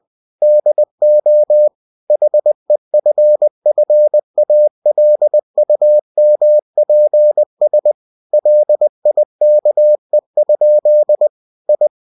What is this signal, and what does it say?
Morse code, heard as a keyed tone, about 20 wpm, 600 Hz. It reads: DO HEFFALUMPS LIKE? I
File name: cw-books / winnie-the-pooh / winnie-20wpm-page-0037